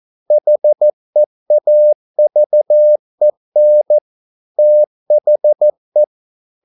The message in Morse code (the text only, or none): HEAVEN THE